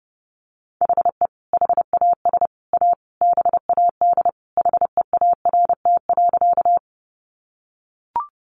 Morse code, heard as a keyed tone, about 30 wpm, 700 Hz. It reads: HE HAS A BAD HEART.
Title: Sentences from Top 1200 Words 30wpm